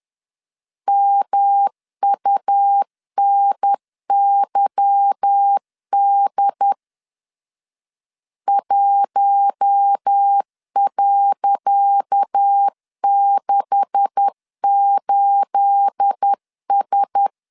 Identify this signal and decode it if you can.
Morse code, heard as a keyed tone, about 10 wpm, 800 Hz. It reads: MUNYD 1.68S
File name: LEZIONI / 10 SIGNS / 10C DOT 10wpm 2940